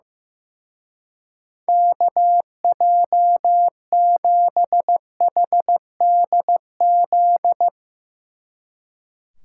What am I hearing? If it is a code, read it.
Morse code, heard as a keyed tone, about 15 wpm, 700 Hz. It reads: KJ7HDZ
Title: Speed Racing - USA Call Signs 15wpm